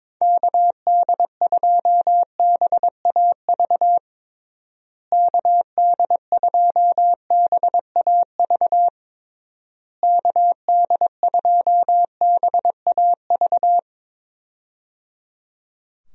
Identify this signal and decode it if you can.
Morse code, heard as a keyed tone, about 22 wpm, 700 Hz. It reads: KD2BAV KD2BAV KD2BAV
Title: USA Call Signs - 6 Characters - Repeated 3x - 22wpm